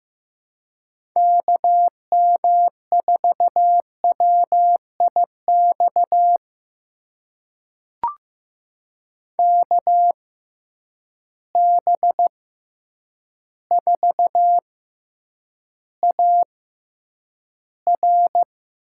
Morse code, heard as a keyed tone, about 15 wpm, spaced as for 8 wpm, 700 Hz. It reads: KM4WIX KB4AR
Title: USA Call Signs - 6 Characters - 6x Character Spacing - 15wpm